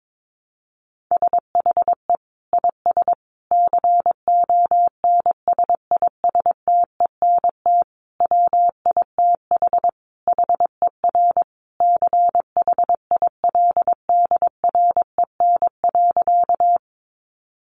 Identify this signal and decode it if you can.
Morse code, heard as a keyed tone, about 22 wpm, 700 Hz. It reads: SHE IS CONSISTENT WITH HER CHILDREN.
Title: Sentences from Top 1500 Words 22wpm